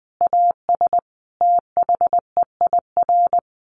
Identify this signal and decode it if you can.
Morse code, heard as a keyed tone, about 20 wpm, 700 Hz. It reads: AS THEIR
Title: Sentences from Top 1500 Words 20wpm